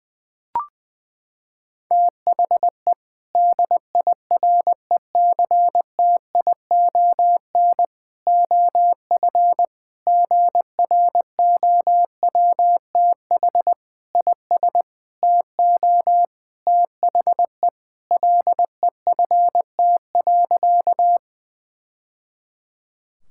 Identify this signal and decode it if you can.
Morse code, heard as a keyed tone, about 20 wpm, 700 Hz. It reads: THE DIRECTION OF GROWTH IS TO THE LEFT.
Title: Sentences from Top 1000 Words - Review Part 5 - 20wpm